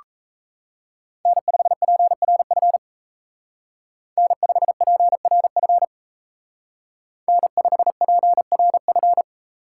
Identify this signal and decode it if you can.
Morse code, heard as a keyed tone, about 33 wpm, 700 Hz. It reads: N5PRF N5PRF N5PRF
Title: Speed Racing - USA 1x3 Call Signs 28wpm